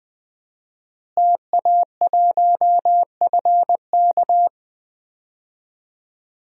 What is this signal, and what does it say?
Morse code, heard as a keyed tone, about 20 wpm, 700 Hz. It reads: TA1FK
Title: DXCC-100 Call Signs - 5 Characters - Repeated 3x - 20wpm